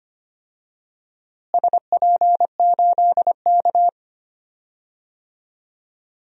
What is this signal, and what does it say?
Morse code, heard as a keyed tone, about 25 wpm, 700 Hz. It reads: SP8K
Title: DXCC-100 Call Signs - 4 Characters - Repeated 2x - 25wpm